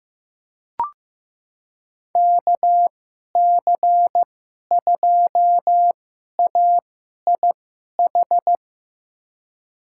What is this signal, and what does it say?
Morse code, heard as a keyed tone, about 15 wpm, spaced as for 11 wpm, 700 Hz. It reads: KC2AIH
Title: USA Call Signs - 6 Characters - 2x Character Spacing - 15wpm